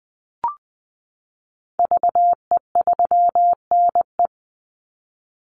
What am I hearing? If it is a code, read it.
Morse code, heard as a keyed tone, about 20 wpm, 700 Hz. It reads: VE3NE
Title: Call Signs Repeated 2x - 20wpm